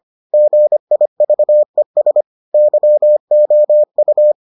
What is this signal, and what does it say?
Morse code, heard as a keyed tone, about 25 wpm, 600 Hz. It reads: GIVES YOU